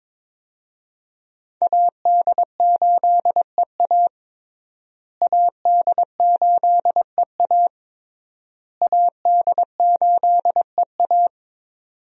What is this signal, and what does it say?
Morse code, heard as a keyed tone, about 22 wpm, 700 Hz. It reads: AD8EA AD8EA AD8EA
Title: USA Call Signs - 5 Characters - Repeated 3x - 22wpm